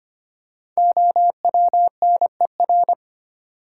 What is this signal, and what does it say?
Morse code, heard as a keyed tone, about 25 wpm, 700 Hz. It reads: OWNER
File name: 5 Letter Words - First 3 Letters Given - 25wpm